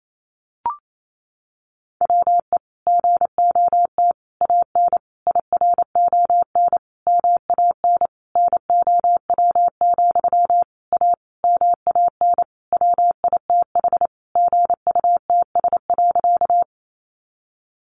Morse code, heard as a keyed tone, about 28 wpm, 700 Hz. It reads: WE GOT AN IRON MAN NOW, A MAN WITH GUTS.
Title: A Sound Of Thunder 28wpm